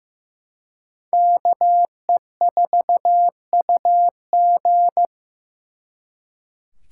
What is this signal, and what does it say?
Morse code, heard as a keyed tone, about 18 wpm, 700 Hz. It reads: KE4UG
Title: Speed Racing - USA 2x2 Call Signs 15wpm